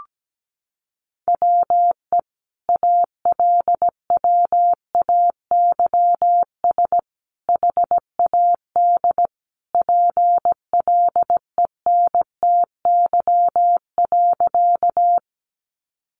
Morse code, heard as a keyed tone, about 17 wpm, 700 Hz. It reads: WE ALWAYS HAD PLENTY.
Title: Sentences from Top 1300 Words 17wpm